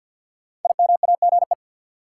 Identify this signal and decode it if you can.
Morse code, heard as a keyed tone, about 50 wpm, 700 Hz. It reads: IMAGE
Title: Sentences from Top 700 Words - New Word Drill - 50wpm